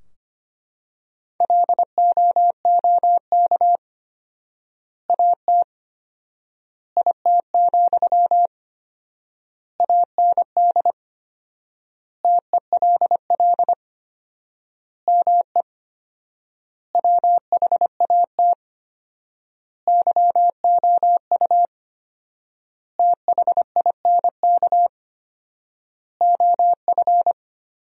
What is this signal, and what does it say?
Morse code, heard as a keyed tone, about 25 wpm, 700 Hz. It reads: LOOK AT IT, AND TELL ME WHAT YOU THINK OF
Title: 4x Word-Spacing - Sentences from Top 200 Words - 25wpm